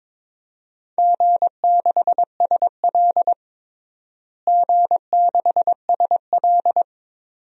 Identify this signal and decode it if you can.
Morse code, heard as a keyed tone, about 22 wpm, 700 Hz. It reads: G6SL G6SL
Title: DXCC-100 Call Signs - 4 Characters - Repeated 2x - 22wpm